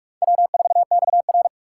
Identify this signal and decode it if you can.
Morse code, heard as a keyed tone, about 42 wpm, 700 Hz. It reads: W4XR
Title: USA Call Signs - 4 Characters - 4x Character Spacing - 45wpm